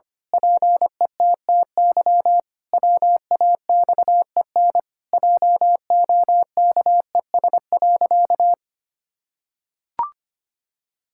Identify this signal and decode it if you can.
Morse code, heard as a keyed tone, about 25 wpm, 700 Hz. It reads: PETTY WAXEN JOKES.
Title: Pangram Challenge v2 - 25wpm